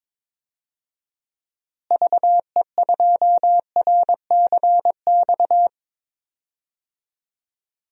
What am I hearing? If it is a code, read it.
Morse code, heard as a keyed tone, about 21 wpm, 700 Hz. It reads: VE2RCX
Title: DXCC-100 Call Signs - 6 Characters - Repeated 3x - 22wpm